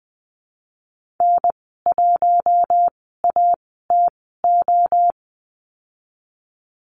Morse code, heard as a keyed tone, about 20 wpm, spaced as for 14 wpm, 700 Hz. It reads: N1ATO
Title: 076 Lesson 13g - Letter L - Call Signs 20wpm